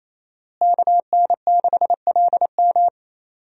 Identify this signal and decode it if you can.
Morse code, heard as a keyed tone, about 27 wpm, 700 Hz. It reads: KN6LM
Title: USA Call Signs - 5 Characters - Repeated 3x - 28wpm